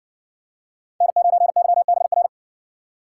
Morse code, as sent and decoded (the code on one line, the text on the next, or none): -. ----. -.-- -... .-.
N9YBR